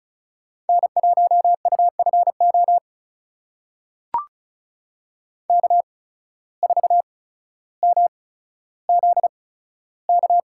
Morse code, heard as a keyed tone, about 34 wpm, spaced as for 15 wpm, 700 Hz. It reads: N1UFO K4MZK
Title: USA Call Signs - 5 Characters - 8x Character Spacing - 35wpm